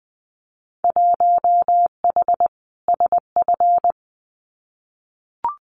Morse code, heard as a keyed tone, about 20 wpm, 700 Hz. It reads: 1H SF
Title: Winter Field Day Exchanges 20wpm